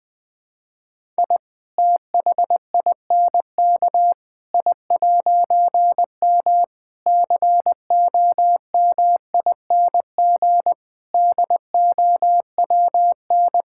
Morse code, heard as a keyed tone, about 20 wpm, 700 Hz. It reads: I THINK I'M COMING DOWN
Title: All-Age Appropriate Jokes - 20wpm